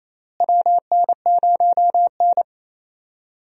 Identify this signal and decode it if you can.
Morse code, heard as a keyed tone, about 27 wpm, 700 Hz. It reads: WN0N
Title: USA Call Signs - 4 Characters - Repeated 2x - 28wpm